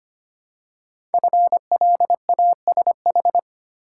Morse code, heard as a keyed tone, about 25 wpm, 700 Hz. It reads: FLASH
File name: 5 Letter Words - First 3 Letters Given - 25wpm